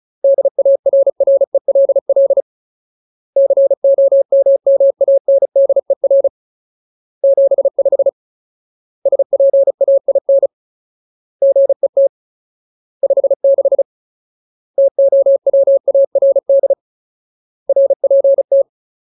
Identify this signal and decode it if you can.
Morse code, heard as a keyed tone, about 34 wpm, 550 Hz. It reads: DARRELL COMMANDER 75 SPAIN GET 56 TOWARD RPT
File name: Ultimate Word List / 35WPM Ultimate